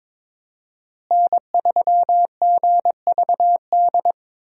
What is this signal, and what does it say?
Morse code, heard as a keyed tone, about 22 wpm, 700 Hz. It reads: N3GVD
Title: USA Call Signs - 5 Characters - Repeated 2x - 22wpm